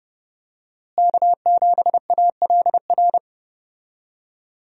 Morse code, heard as a keyed tone, about 30 wpm, 700 Hz. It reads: K7ALR
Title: USA Call Signs - 5 Characters - Repeated 2x - 30wpm